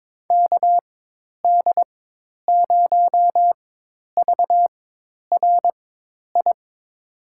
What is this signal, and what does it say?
Morse code, heard as a keyed tone, about 22 wpm, spaced as for 10 wpm, 700 Hz. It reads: KD0VRI
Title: USA Call Signs - 6 Characters - 4x Character Spacing - 22wpm